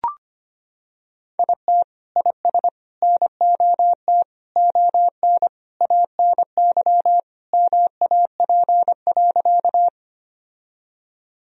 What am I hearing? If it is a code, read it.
Morse code, heard as a keyed tone, about 25 wpm, 700 Hz. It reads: IT IS NOT ON ANY MAP.
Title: Sentences from Top 1000 Words - Review Part 2 - 25wpm